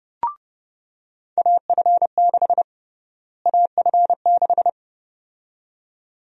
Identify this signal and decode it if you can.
Morse code, heard as a keyed tone, about 30 wpm, 700 Hz. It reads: AF6 AF6
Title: USA Call Sign Prefixes - Repeated 2x - 30wpm